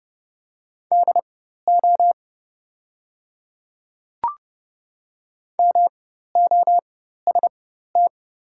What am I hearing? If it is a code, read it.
Morse code, heard as a keyed tone, about 30 wpm, spaced as for 14 wpm, 700 Hz. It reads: DO MOST
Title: 4x Character Spacing - Top 100 Words - 30wpm